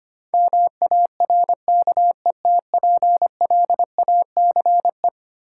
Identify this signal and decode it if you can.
Morse code, heard as a keyed tone, about 25 wpm, 700 Hz. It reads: MARKETPLACE
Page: morse-code-ninja-practice-25wpm